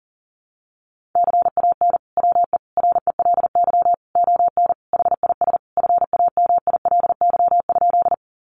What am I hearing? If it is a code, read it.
Morse code, heard as a keyed tone, about 40 wpm, 700 Hz. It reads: CAN WE RELY ON HIS FAMILY?
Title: Sentences from Top 1200 Words 40wpm